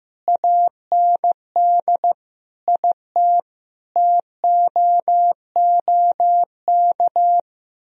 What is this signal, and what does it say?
Morse code, heard as a keyed tone, about 15 wpm, 700 Hz. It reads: AND IT TOOK
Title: Sentences from Top 1000 Words - Review Part 4 - 15wpm